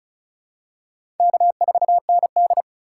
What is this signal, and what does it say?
Morse code, heard as a keyed tone, about 36 wpm, 700 Hz. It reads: K4ND
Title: USA Call Signs - 2x Character Spacing - 35wpm